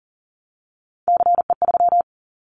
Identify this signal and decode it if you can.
Morse code, heard as a keyed tone, about 40 wpm, 700 Hz. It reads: CE3
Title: Callsign Prefixes for Top 100 DXCC - 40wpm